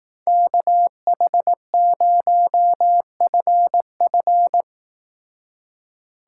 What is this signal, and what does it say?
Morse code, heard as a keyed tone, about 18 wpm, 700 Hz. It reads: KH0FF
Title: Speed Racing - USA 2x2 Call Signs 15wpm